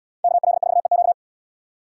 Morse code, heard as a keyed tone, about 71 wpm, 700 Hz. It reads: KL4EQ